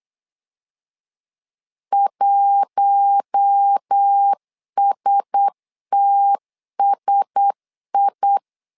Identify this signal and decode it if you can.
Morse code, heard as a keyed tone, about 8 wpm, 800 Hz. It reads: 1STSI
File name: LEZIONI / 4 Q1S 8wpm 2710